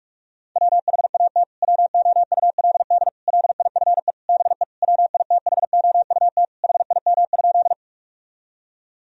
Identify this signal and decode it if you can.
Morse code, heard as a keyed tone, about 46 wpm, 700 Hz. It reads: WHAT WOULD LIFE BE WITHOUT HIM?
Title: Sentences from Top 1000 Words - Review Part 3 - 45wpm